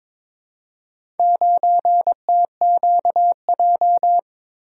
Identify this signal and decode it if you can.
Morse code, heard as a keyed tone, about 23 wpm, 700 Hz. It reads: 9TQJ
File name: practice / 22wpm / Four Letter-Number Combinations - 22wpm